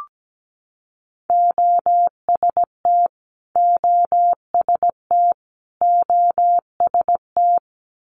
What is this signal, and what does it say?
Morse code, heard as a keyed tone, about 17 wpm, 700 Hz. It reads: OST OST OST
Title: Multi-letter Phonemes Repeated 3x 17wpm